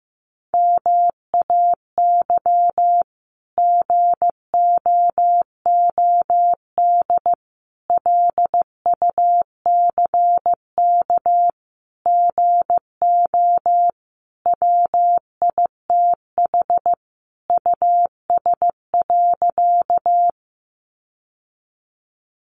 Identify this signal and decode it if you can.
Morse code, heard as a keyed tone, about 15 wpm, 700 Hz. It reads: MAY GOOD LUCK GO WITH US.
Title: Sentences from Top 1500 Words 15wpm